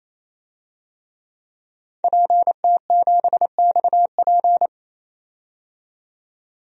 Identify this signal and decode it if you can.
Morse code, heard as a keyed tone, about 28 wpm, 700 Hz. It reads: PT7XP
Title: DXCC-100 Call Signs - 5 Characters - 28wpm